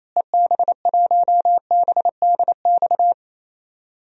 Morse code, heard as a keyed tone, about 28 wpm, 700 Hz. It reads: EB1BDX
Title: DXCC-100 Call Signs - 6 Characters - Repeated 2x - 28wpm